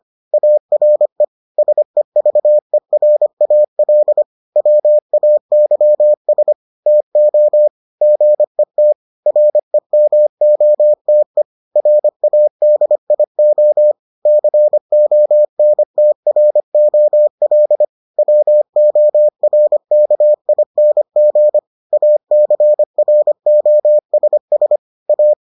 Morse code, heard as a keyed tone, about 25 wpm, 600 Hz. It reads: ARE SEVERAL WAYS TO GET REMOTE RADIO CONTROL WORKING ACROSS A